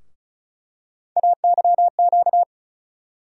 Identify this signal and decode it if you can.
Morse code, heard as a keyed tone, about 36 wpm, 700 Hz. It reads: AYQ
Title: USA Call Sign Postfixes - 3 Letters 35wpm